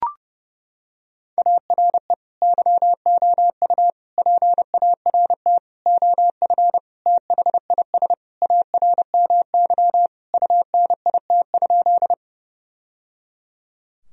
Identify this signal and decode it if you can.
Morse code, heard as a keyed tone, about 30 wpm, 700 Hz. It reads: ARE YOU PART OF THIS ARMY UNIT?
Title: Sentences from Top 1000 Words - Review Part 2 - 30wpm